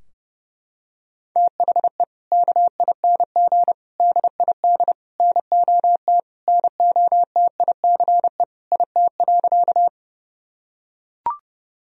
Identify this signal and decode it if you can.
Morse code, heard as a keyed tone, about 30 wpm, 700 Hz. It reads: THE KING DID NOT NOTICE IT.
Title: Sentences from Top 1000 Words - Review Part 2 - 30wpm